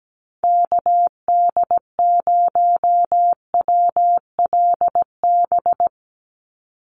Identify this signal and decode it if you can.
Morse code, heard as a keyed tone, about 17 wpm, 700 Hz. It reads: KD0WLB
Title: USA Call Signs 17wpm